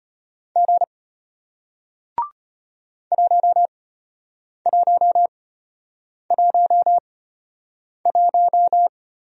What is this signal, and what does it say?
Morse code, heard as a keyed tone, about 36 wpm, spaced as for 8 wpm, 700 Hz. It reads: G 1111